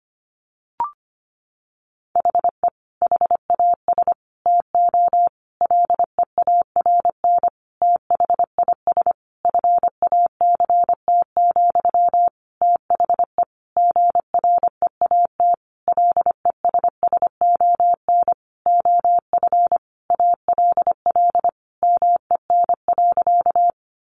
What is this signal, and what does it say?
Morse code, heard as a keyed tone, about 25 wpm, 700 Hz. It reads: HE HAS TO LEARN THIS FACT, THE GREAT LESSON OF ALL MEN.
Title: Sentences from Top 1200 Words 25wpm